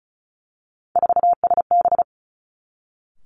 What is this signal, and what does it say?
Morse code, heard as a keyed tone, about 36 wpm, 700 Hz. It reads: 4SB
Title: Three Challenging Letter-Number Combinations 35wpm